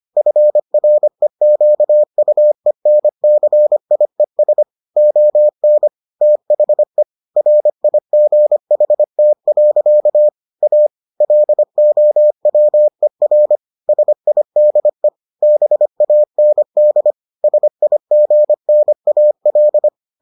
Morse code, heard as a keyed tone, about 25 wpm, 600 Hz. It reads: FREQUENCIES ON THE RIGHT. A LOWER SIDE BAND SIGNAL